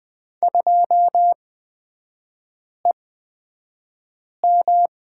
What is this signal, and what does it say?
Morse code, heard as a keyed tone, about 20 wpm, spaced as for 5 wpm, 700 Hz. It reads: 2EM
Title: 255 Lesson 41e - Period - Three Characters 20wpm